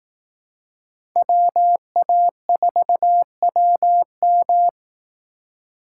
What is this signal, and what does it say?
Morse code, heard as a keyed tone, about 18 wpm, 700 Hz. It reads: WA4WM
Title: Speed Racing - USA 2x2 Call Signs 15wpm